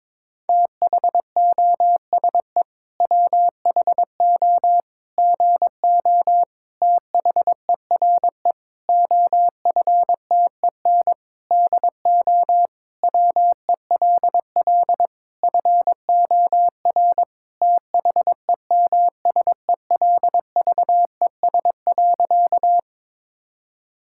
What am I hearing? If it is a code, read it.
Morse code, heard as a keyed tone, about 22 wpm, 700 Hz. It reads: THOSE WHO GO THERE OFTEN DO WELL FOR THEMSELVES.
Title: Sentences from Top 1000 Words - Review Part 5 - 22wpm